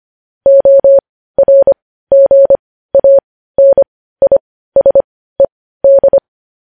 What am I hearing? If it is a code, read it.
Morse code, heard as a keyed tone, about 25 wpm, spaced as for 15 wpm, 550 Hz. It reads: ORGANISED
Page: news-headlines-in-morse-code-at-15-wpm